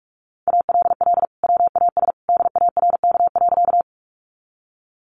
Morse code, heard as a keyed tone, about 44 wpm, 700 Hz. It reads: ALL WAS DARK.
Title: Sentences from Top 500 Words 45wpm